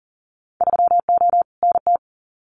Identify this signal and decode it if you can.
Morse code, heard as a keyed tone, about 40 wpm, 700 Hz. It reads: 3O NT
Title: Winter Field Day Exchanges Encore 40wpm